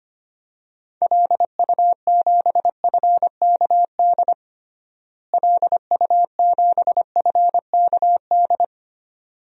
Morse code, heard as a keyed tone, about 25 wpm, 700 Hz. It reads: LU7FKD LU7FKD
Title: DXCC-100 Call Signs - 6 Characters - Repeated 2x - 25wpm